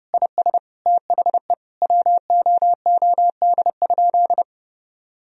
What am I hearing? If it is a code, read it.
Morse code, heard as a keyed tone, about 30 wpm, 700 Hz. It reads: IS THE WOOD?
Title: Sentences from Top 1000 Words - Review Part 2 - 30wpm